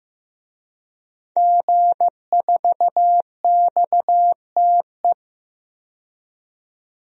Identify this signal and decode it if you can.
Morse code, heard as a keyed tone, about 15 wpm, 700 Hz. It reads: G4XTE
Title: DXCC-100 Call Signs - 5 Characters - 15wpm